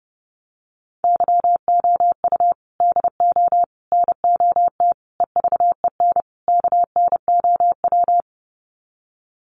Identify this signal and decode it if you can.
Morse code, heard as a keyed tone, about 30 wpm, 700 Hz. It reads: YOU DO NOT EVEN KNOW
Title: Sets of 5 Words Encore2 30wpm